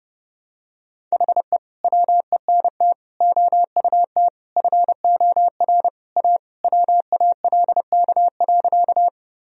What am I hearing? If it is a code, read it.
Morse code, heard as a keyed tone, about 32 wpm, 700 Hz. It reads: HE WENT OUT FOR A WALK.
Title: Sentences from Top 1000 Words - Review Part 1 - 30wpm